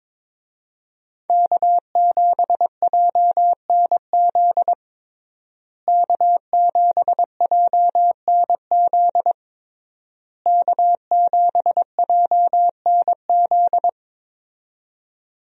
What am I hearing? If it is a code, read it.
Morse code, heard as a keyed tone, about 22 wpm, 700 Hz. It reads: K7JNZ K7JNZ K7JNZ
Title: USA Call Signs - 5 Characters - Repeated 3x - 22wpm